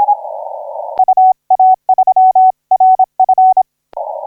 Morse code, heard as a keyed tone, about 26 wpm, 750 Hz. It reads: UA3RF
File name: D85C 0845UTC 15m